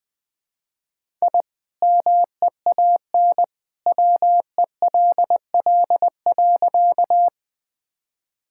Morse code, heard as a keyed tone, about 20 wpm, 700 Hz. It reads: I MEAN WELL.
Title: Sentences with 2-Word Phrases - Spoken Beginning - 20wpm